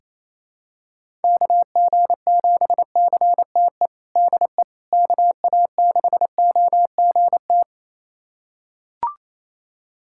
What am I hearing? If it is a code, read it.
Morse code, heard as a keyed tone, about 27 wpm, 700 Hz. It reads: KG7CTE DE KA6OGT
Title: USA Callsign DE Callsign - 6 Characters - 28wpm